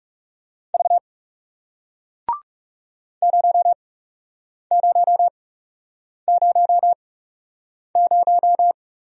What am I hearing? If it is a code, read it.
Morse code, heard as a keyed tone, about 38 wpm, spaced as for 8 wpm, 700 Hz. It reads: V 0000